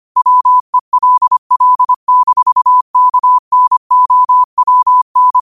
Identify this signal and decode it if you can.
Morse code, heard as a keyed tone, about 25 wpm, 1000 Hz. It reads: WELL-KNOWN
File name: qotdfiles / 25wpm